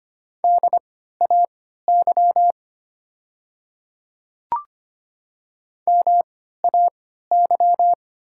Morse code, heard as a keyed tone, about 25 wpm, spaced as for 14 wpm, 700 Hz. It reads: DAY MAY
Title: 3x Character Spacing - Top 100 Words - 25wpm